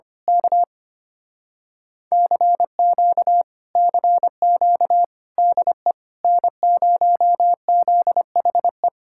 Morse code, heard as a keyed tone, about 24 wpm, 700 Hz. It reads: K CQ CQ DE N0ZHE
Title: Speed-Racing Common QSO Phrases Encore - 25wpm